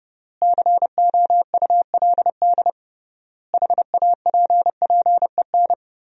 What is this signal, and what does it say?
Morse code, heard as a keyed tone, about 30 wpm, 700 Hz. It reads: COULD HAPPEN
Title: 3x Word-Spacing - Sentences from Top 400 Words - 30wpm